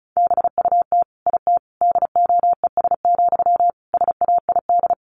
Morse code, heard as a keyed tone, about 34 wpm, 700 Hz. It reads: BUT IT DOES, SAID
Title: Sentences from Top 1500 Words 35wpm